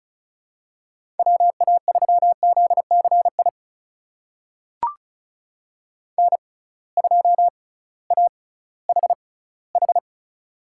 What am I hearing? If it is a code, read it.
Morse code, heard as a keyed tone, about 34 wpm, spaced as for 19 wpm, 700 Hz. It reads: WA3ZCI N2AHH
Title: USA Call Signs - 6x Character Spacing - 35wpm